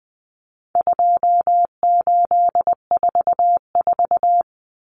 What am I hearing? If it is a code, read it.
Morse code, heard as a keyed tone, about 20 wpm, 700 Hz. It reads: 2844
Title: Numbers - 4-Digit 20wpm